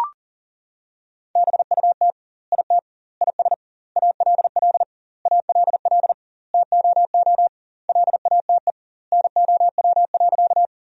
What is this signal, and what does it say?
Morse code, heard as a keyed tone, about 40 wpm, 700 Hz. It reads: BUT IT IS ALL ALL TOO LATE NOW.
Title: 2x Word-Spacing - Sentences from Top 300 Words - 40wpm